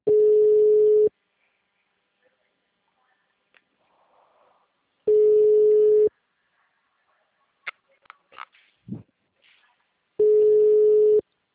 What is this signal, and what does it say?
Ringback tone (Europe, 425 Hz).
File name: out- -2011- - - .74997